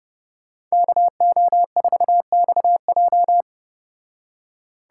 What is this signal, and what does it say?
Morse code, heard as a keyed tone, about 34 wpm, 700 Hz. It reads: KO4XJ